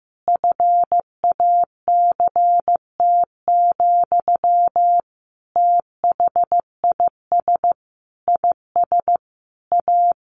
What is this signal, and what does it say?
Morse code, heard as a keyed tone, about 15 wpm, 700 Hz. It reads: FACT, THIS IS A
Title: Sentences from Top 1500 Words 15wpm